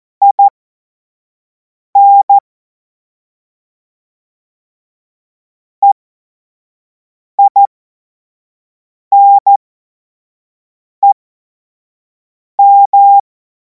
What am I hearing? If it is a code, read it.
Morse code, heard as a keyed tone, about 14 wpm, spaced as for 5 wpm, 800 Hz. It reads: IN EINEM
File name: Morsekurs / Witze / Witz 02. Juni 2026 25 F70